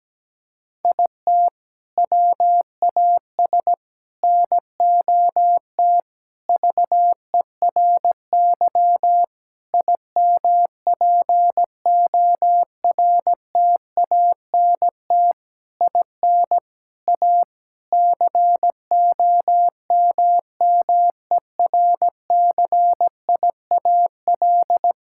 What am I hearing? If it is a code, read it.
Morse code, heard as a keyed tone, about 17 wpm, 700 Hz. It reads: IT WAS NOT VERY IMPORTANT IN A COMMERCIAL
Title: Sentences from Top 1000 Words - Review Part 5 - 17wpm